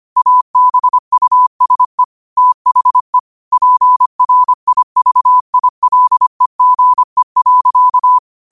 Morse code, heard as a keyed tone, about 25 wpm, 1000 Hz. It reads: ABUSE THE PRIVILEGE.
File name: qotdfiles / 25wpm